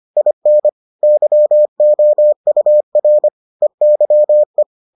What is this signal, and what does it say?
Morse code, heard as a keyed tone, about 25 wpm, 600 Hz. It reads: IN YOUR EYE